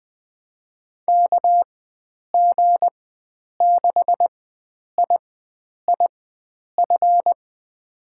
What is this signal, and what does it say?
Morse code, heard as a keyed tone, about 20 wpm, spaced as for 9 wpm, 700 Hz. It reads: KG6IIF
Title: USA Call Signs - 6 Characters - 4x Character Spacing - 20wpm